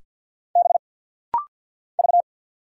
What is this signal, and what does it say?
Morse code, heard as a keyed tone, about 50 wpm, 700 Hz. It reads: B V
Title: Single Letter-Number - Rapid-Fire 50wpm